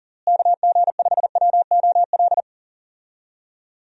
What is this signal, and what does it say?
Morse code, heard as a keyed tone, about 40 wpm, 700 Hz. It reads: KG5WOL
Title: USA Call Signs - 6 Characters - Repeated 3x - 40wpm